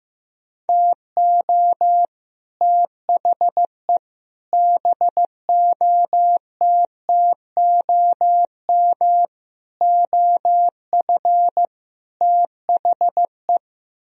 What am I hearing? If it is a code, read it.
Morse code, heard as a keyed tone, about 15 wpm, 700 Hz. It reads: TO THE BOTTOM OF THE
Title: Speed-Racing - Sets of 5 Words 15wpm